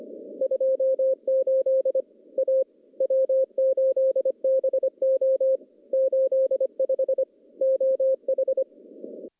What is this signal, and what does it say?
Morse code, heard as a keyed tone, about 25 wpm, 530 Hz. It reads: 28 A W8BO 85 OH
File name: contests / CW-SS-k5tr- -0400